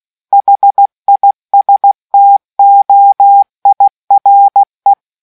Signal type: Morse code